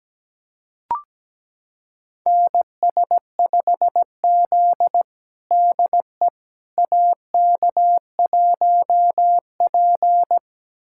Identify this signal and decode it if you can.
Morse code, heard as a keyed tone, about 17 wpm, 700 Hz. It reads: NS5Z DE AK1P
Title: USA Callsign DE Callsign - 4 Characters - 17wpm